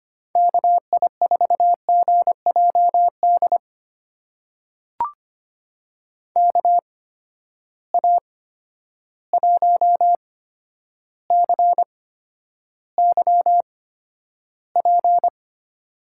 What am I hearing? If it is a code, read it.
Morse code, heard as a keyed tone, about 25 wpm, spaced as for 11 wpm, 700 Hz. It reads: KI4GJD KA1CYP
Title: USA Call Signs - 6 Characters - 8x Character Spacing - 25wpm